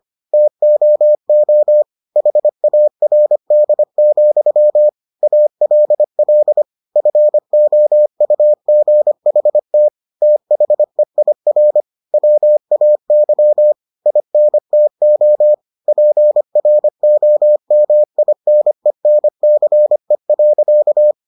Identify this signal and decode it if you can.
Morse code, heard as a keyed tone, about 25 wpm, 600 Hz. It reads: TOO HARD, ALL FOUGHT THEIR WAY INTO PROMINENCE.